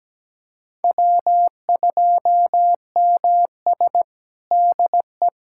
Morse code, heard as a keyed tone, about 17 wpm, 700 Hz. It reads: W2MS DE
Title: USA Callsign DE Callsign - 4 Characters - 17wpm